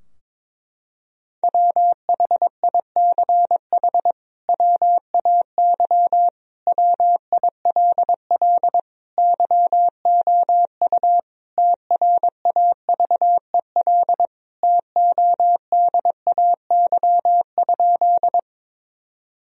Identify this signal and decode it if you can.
Morse code, heard as a keyed tone, about 23 wpm, 700 Hz. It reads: WHICH WAY WILL YOU TRAVEL TODAY?
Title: Sentences from Top 1000 Words - Review Part 2 - 22wpm